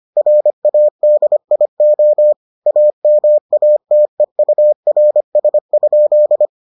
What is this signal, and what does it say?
Morse code, heard as a keyed tone, about 25 wpm, 600 Hz. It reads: RADIO AMATEURS?